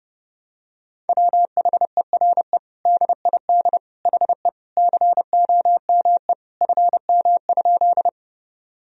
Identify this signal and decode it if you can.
Morse code, heard as a keyed tone, about 30 wpm, 700 Hz. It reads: WHERE DID HE COME FM?
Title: Sentences from Top 200 Words and Abbreviations 30wpm